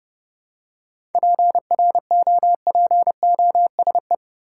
Morse code, heard as a keyed tone, about 32 wpm, 700 Hz. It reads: PROPOSE
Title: Seven Letter Words - 30wpm